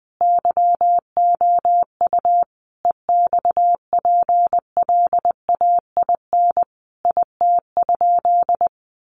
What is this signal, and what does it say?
Morse code, heard as a keyed tone, about 20 wpm, 700 Hz. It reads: YOU EXPLAIN IT?
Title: Sentences from Top 1100 Words 20wpm